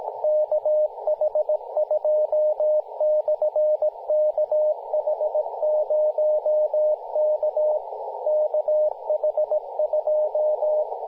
Morse code, heard as a keyed tone, about 17 wpm, 610 Hz, with radio noise. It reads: KH2/KH0K KH2
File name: kh2-kh0k-18-082mhz-cw